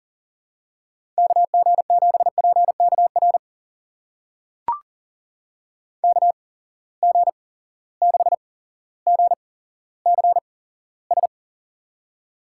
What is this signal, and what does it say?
Morse code, heard as a keyed tone, about 40 wpm, spaced as for 17 wpm, 700 Hz. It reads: KG7PKR KG6GCS
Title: USA Call Signs - 6 Characters - 8x Character Spacing - 40wpm